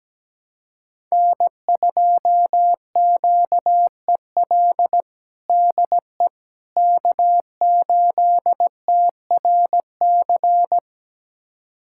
Morse code, heard as a keyed tone, about 17 wpm, 700 Hz. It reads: N2QEL DE K8TRC
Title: USA Callsign DE Callsign - 5 Characters - 17wpm